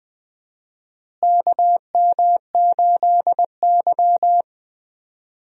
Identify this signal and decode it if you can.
Morse code, heard as a keyed tone, about 20 wpm, 700 Hz. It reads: KM8Y
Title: USA Call Signs - 4 Characters - Repeated 2x - 20wpm